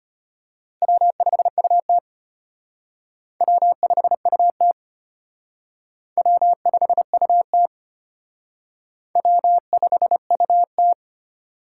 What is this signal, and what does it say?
Morse code, heard as a keyed tone, about 32 wpm, 700 Hz. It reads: W5UT W5UT W5UT W5UT